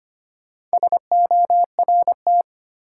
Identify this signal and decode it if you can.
Morse code, heard as a keyed tone, about 25 wpm, 700 Hz. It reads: SORT
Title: Sentences from Top 1300 Words - New Word Drill Alt - 25wpm